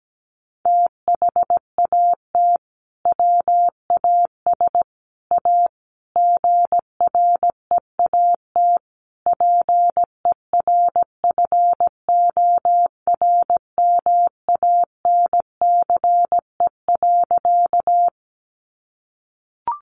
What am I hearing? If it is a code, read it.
Morse code, heard as a keyed tone, about 17 wpm, 700 Hz. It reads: THAT WAS A GREAT PERFORMANCE.
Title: Sentences from Top 1000 Words 17wpm